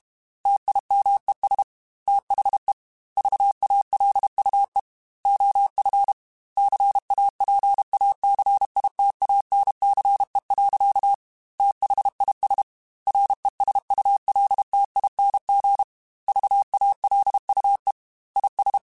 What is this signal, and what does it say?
Morse code, heard as a keyed tone, about 32 wpm, 800 Hz. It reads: TIMES THE VALUE OF CAPACITANCE. THIS RESULTING VALUE IS
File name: Technical Subjects / Technical Subjects-mp3s / XC @30